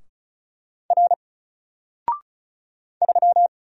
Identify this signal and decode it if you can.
Morse code, heard as a keyed tone, about 36 wpm, 700 Hz. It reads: R 3
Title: Single Character (Letter:Number:Punctuation:Prosign) 35wpm